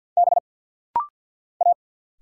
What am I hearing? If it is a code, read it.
Morse code, heard as a keyed tone, about 50 wpm, 700 Hz. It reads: B A
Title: Single Letter-Number - Rapid-Fire 50wpm